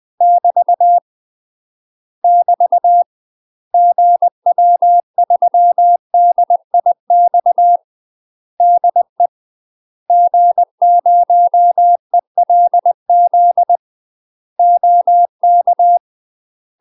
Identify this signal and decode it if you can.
Morse code, heard as a keyed tone, about 20 wpm, 700 Hz. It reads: <BT> <BT> GW3DIX DE G0ELZ OK
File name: UK MASTER QSOs TRAINING / 20WPM / UK MASTER QSOs 0028